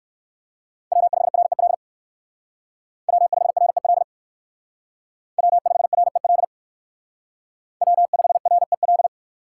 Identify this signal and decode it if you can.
Morse code, heard as a keyed tone, about 59 wpm, 700 Hz. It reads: W5REL W5REL W5REL W5REL